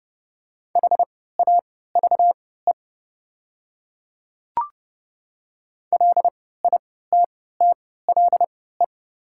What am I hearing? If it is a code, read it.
Morse code, heard as a keyed tone, about 32 wpm, spaced as for 17 wpm, 700 Hz. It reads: HAVE LITTLE
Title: 3x Character Spacing - Top 300 Words - 30wpm